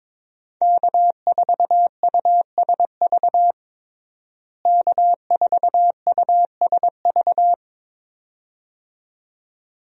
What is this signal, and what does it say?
Morse code, heard as a keyed tone, about 22 wpm, 700 Hz. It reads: K4USV K4USV
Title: USA Call Signs - 5 Characters - Repeated 2x - 22wpm